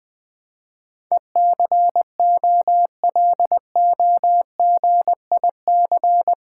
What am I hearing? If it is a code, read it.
Morse code, heard as a keyed tone, about 20 wpm, 700 Hz. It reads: ECOLOGIC
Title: Build A Word - 4 Letters At A Time - 10-Letter Words - 20wpm